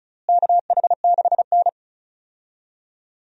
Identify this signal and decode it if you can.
Morse code, heard as a keyed tone, about 34 wpm, 700 Hz. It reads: KH6N
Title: DXCC-100 Call Signs - 4 Characters - Repeated 3x - 35wpm